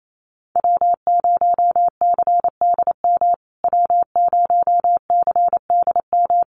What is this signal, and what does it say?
Morse code, heard as a keyed tone, about 28 wpm, 700 Hz. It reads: W0CDM W0CDM
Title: USA Call Signs Repeated 3x 28wpm